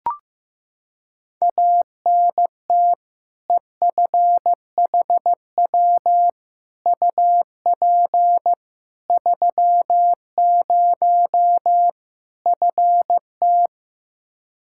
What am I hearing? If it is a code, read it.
Morse code, heard as a keyed tone, about 16 wpm, 700 Hz. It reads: ANT EFHW UP 30 FT
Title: Practice QSO - Exchange 2 - Info - 15wpm